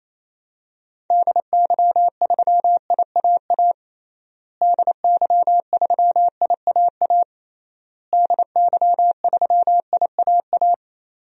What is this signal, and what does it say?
Morse code, heard as a keyed tone, about 28 wpm, 700 Hz. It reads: DY3IAA DY3IAA DY3IAA
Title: DXCC-100 Call Signs - 6 Characters - Repeated 3x - 28wpm